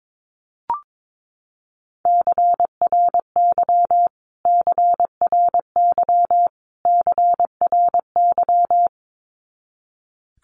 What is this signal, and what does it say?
Morse code, heard as a keyed tone, about 22 wpm, 700 Hz. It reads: CRY CRY CRY
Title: Top 500 Words Repeated 22wpm